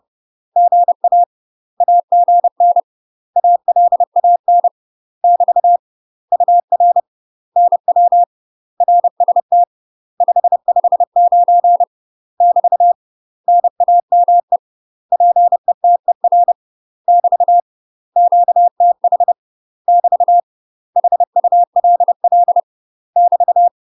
Morse code, heard as a keyed tone, about 30 wpm, 700 Hz. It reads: GA AGN ALAN <BT> UR NW RST 559 <BT> NAME PETER <BT> QTH <BT> HULL <BT>
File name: UK MASTER QSOs TRAINING / 30WPM / UK MASTER QSOs 0033